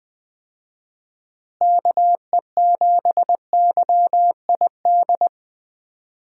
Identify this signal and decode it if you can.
Morse code, heard as a keyed tone, about 21 wpm, 700 Hz. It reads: KE7YID
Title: USA Call Signs - 6 Characters - Repeated 3x - 20wpm